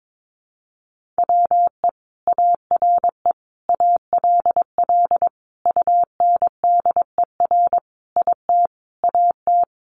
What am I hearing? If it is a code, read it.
Morse code, heard as a keyed tone, about 22 wpm, 700 Hz. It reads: WE ARE ALL UNDER IT AT
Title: Sentences from Top 1500 Words 22wpm